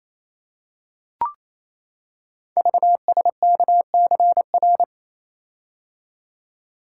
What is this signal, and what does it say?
Morse code, heard as a keyed tone, about 29 wpm, 700 Hz. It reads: VSKCR
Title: Code Groups - 5 Letters - 28wpm